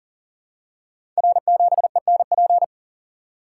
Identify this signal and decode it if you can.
Morse code, heard as a keyed tone, about 39 wpm, 700 Hz. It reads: R7ENP
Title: Code Groups - 5 Letter:Numbers - 40wpm